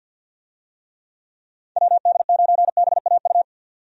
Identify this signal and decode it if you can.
Morse code, heard as a keyed tone, about 53 wpm, 700 Hz. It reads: WD9BAU
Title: USA Call Signs - 6 Characters - 50wpm